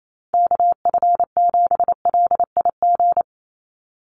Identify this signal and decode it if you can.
Morse code, heard as a keyed tone, about 28 wpm, 700 Hz. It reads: KF7LIG
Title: USA Call Signs 28wpm